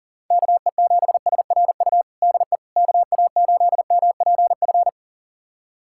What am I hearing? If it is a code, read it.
Morse code, heard as a keyed tone, about 40 wpm, 700 Hz. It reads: KE7SRU DE KA8MPF
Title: USA Callsign DE Callsign 40wpm